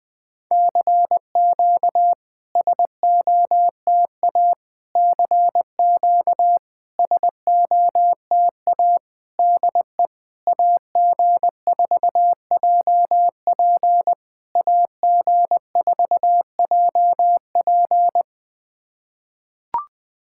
Morse code, heard as a keyed tone, about 18 wpm, 700 Hz. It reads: CQ SOTA CQ SOTA DE AG4JP AG4JP
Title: SOTA QSO Practice 20wpm